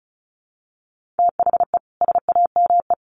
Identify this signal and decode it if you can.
Morse code, heard as a keyed tone, about 34 wpm, 700 Hz. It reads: THE SAME
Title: Sets of 3 Words 35wpm